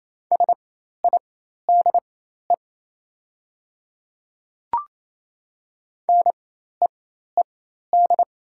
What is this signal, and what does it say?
Morse code, heard as a keyed tone, about 28 wpm, spaced as for 13 wpm, 700 Hz. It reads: SIDE NEED
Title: 4x Character Spacing - Top 300 Words - 28wpm